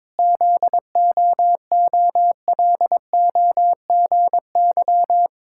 Morse code, heard as a keyed tone, about 22 wpm, 700 Hz. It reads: ZOOLOGY
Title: Pangram Challenge v2 - 22wpm